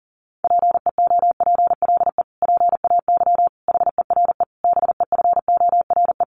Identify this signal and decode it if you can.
Morse code, heard as a keyed tone, about 40 wpm, 700 Hz. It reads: PEOPLE PAY HERE BEFORE
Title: Sentences from Top 700 Words 40wpm